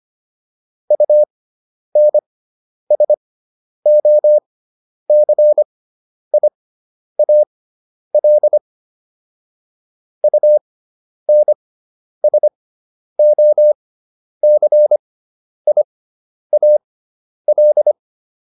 Morse code, heard as a keyed tone, about 25 wpm, spaced as for 10 wpm, 600 Hz. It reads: UNSOCIAL UNSOCIAL